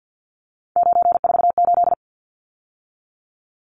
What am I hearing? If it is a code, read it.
Morse code, heard as a keyed tone, about 48 wpm, 700 Hz. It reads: 948
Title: Numbers - 3-Digit 50wpm